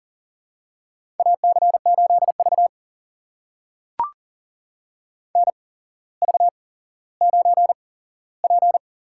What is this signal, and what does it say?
Morse code, heard as a keyed tone, about 42 wpm, spaced as for 17 wpm, 700 Hz. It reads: AC8V NV9P
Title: USA Call Signs - 4 Characters - 8x Character Spacing - 40wpm